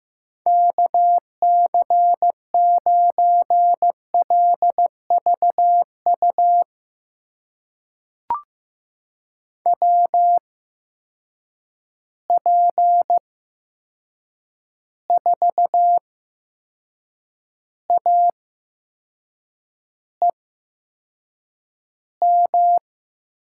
Morse code, heard as a keyed tone, about 15 wpm, spaced as for 6 wpm, 700 Hz. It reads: KC9LVU WP4AEM
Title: USA Call Signs - 6 Characters - 8x Character Spacing - 15wpm